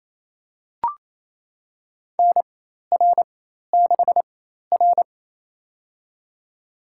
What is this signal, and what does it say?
Morse code, heard as a keyed tone, about 28 wpm, spaced as for 13 wpm, 700 Hz. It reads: NR6R
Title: USA Call Signs - 4 Characters - 4x Character Spacing - 28wpm